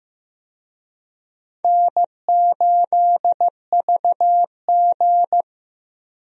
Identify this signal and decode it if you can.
Morse code, heard as a keyed tone, about 15 wpm, 700 Hz. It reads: N8VG
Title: USA Call Signs - 4 Characters - Repeated 2x - 15wpm